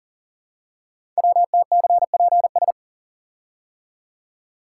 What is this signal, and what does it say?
Morse code, heard as a keyed tone, about 39 wpm, 690 Hz. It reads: WTCPS
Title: Code Groups - 5 Letters - 40wpm